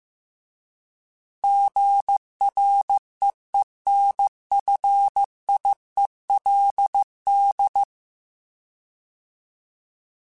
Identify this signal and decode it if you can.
Morse code, heard as a keyed tone, about 15 wpm, 800 Hz. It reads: GREENFIELD